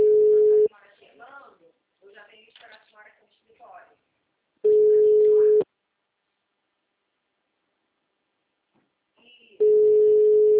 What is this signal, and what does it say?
Ringback tone (Europe, 425 Hz), with speech.